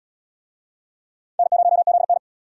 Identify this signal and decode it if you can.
Morse code, heard as a keyed tone, about 74 wpm, 700 Hz. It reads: N0ZN